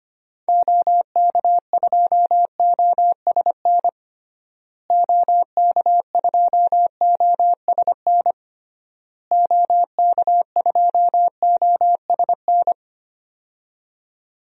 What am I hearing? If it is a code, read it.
Morse code, heard as a keyed tone, about 25 wpm, 700 Hz. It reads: OK2OSN OK2OSN OK2OSN
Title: DXCC-100 Call Signs - 6 Characters - Repeated 3x - 25wpm